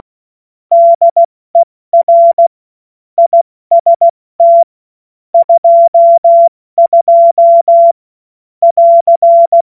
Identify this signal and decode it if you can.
Morse code, heard as a keyed tone, about 16 wpm, spaced as for 14 wpm, 680 Hz. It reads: DER IST 22 <AR>
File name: Morsekurs / Witze / Witz 13. September 2022 70 F80